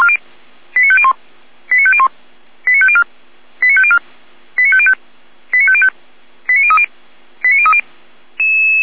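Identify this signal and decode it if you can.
Five-tone selective call (ZVEI), heard as quick runs of five tones, tones of about 70 ms. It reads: 89501 89501 89504 89504 89505 89505 89030 89030
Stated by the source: ZVEI five-tone selective call 89033 received via SDR.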